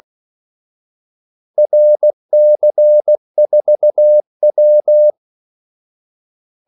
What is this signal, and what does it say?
Morse code, heard as a keyed tone, about 16 wpm, 600 Hz. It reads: RC4W
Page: Kaldesignaler 16 wpm